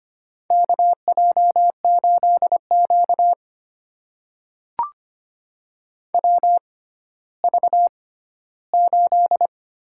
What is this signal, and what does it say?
Morse code, heard as a keyed tone, about 25 wpm, spaced as for 14 wpm, 700 Hz. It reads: KJ8Q WV8
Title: USA Call Signs - 4 Characters - 6x Character Spacing - 25wpm